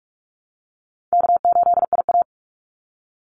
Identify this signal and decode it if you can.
Morse code, heard as a keyed tone, about 44 wpm, 700 Hz. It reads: K8IA
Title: Easy Call Signs Repeated 2x - 45wpm